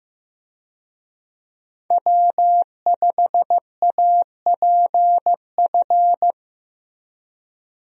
Morse code, heard as a keyed tone, about 15 wpm, 700 Hz. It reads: W5APF
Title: USA Call Signs - 5 Characters - Repeated 3x - 15wpm